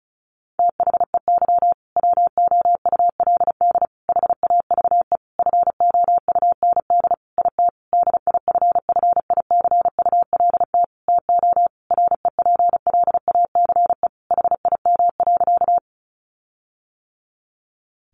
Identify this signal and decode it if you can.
Morse code, heard as a keyed tone, about 34 wpm, 700 Hz. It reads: THEY WOULD HAVE FOUND IT DIFFICULT TO REPLACE HIM.
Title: Sentences from Top 1400 Words 35wpm